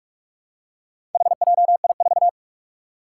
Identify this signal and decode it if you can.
Morse code, heard as a keyed tone, about 42 wpm, 700 Hz. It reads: HJI4
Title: Four Challenging Letter-Number Combinations 45wpm